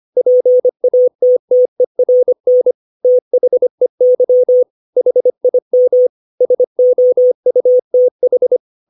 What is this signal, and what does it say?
Morse code, heard as a keyed tone, about 25 wpm, 500 Hz. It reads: PATTERN THEY HIM SOUTH
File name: 632 words / 632 Words 25WPM